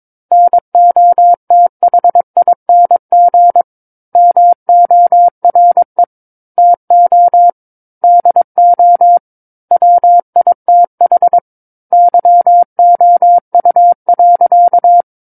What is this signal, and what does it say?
Morse code, heard as a keyed tone, about 22 wpm, 700 Hz. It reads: NOTHING MORE TO DO WITH YOU.
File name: 06-Chapter-06 22 WPM